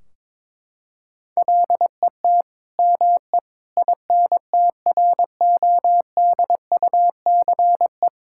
Morse code, heard as a keyed tone, about 21 wpm, 700 Hz. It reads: LET ME INTRODUCE
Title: Sentences from Top 1600 Words 22wpm